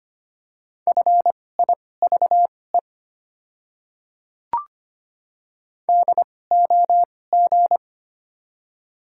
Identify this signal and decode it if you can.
Morse code, heard as a keyed tone, about 25 wpm, spaced as for 18 wpm, 700 Hz. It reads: FIVE DOG
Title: 2x Character Spacing - Top 500 Words - 25wpm